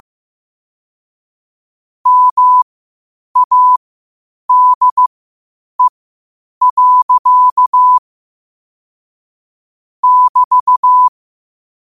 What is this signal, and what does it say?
Morse code, heard as a keyed tone, about 15 wpm, spaced as for 8 wpm, 1000 Hz. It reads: MADE. <BT>
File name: qotdfiles / 8wpm